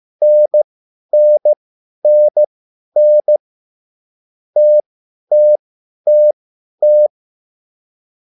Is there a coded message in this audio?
Yes: Morse code from 0.2 to 7.1 s, heard as a keyed tone, about 15 wpm, spaced as for 10 wpm, 600 Hz. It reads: NNNN TTTT